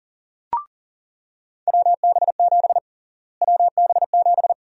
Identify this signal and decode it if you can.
Morse code, heard as a keyed tone, about 40 wpm, 700 Hz. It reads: WB7 WB7
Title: USA Call Sign Prefixes - Repeated 2x - 40wpm